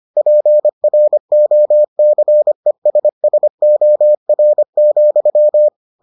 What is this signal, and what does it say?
Morse code, heard as a keyed tone, about 25 wpm, 600 Hz. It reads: PROCESSOR,